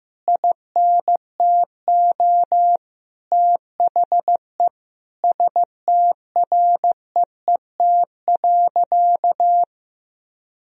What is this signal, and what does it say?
Morse code, heard as a keyed tone, about 15 wpm, 700 Hz. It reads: INTO THE STREET.
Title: Sentences from Top 1000 Words - Review Part 3 - 15wpm